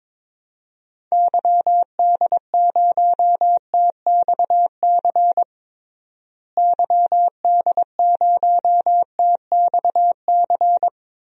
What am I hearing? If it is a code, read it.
Morse code, heard as a keyed tone, about 21 wpm, 700 Hz. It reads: YD0TXC YD0TXC
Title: DXCC-100 Call Signs - 6 Characters - Repeated 3x - 22wpm